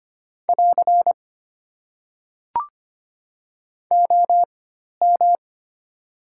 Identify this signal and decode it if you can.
Morse code, heard as a keyed tone, about 25 wpm, spaced as for 12 wpm, 690 Hz. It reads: <AR> OM
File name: Top 90 QSO Elements - 4x Character Spacing - 25wpm